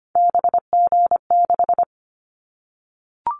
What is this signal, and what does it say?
Morse code, heard as a keyed tone, about 25 wpm, 700 Hz. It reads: BG6
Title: Callsign Prefixes for Top 100 DXCC - 25wpm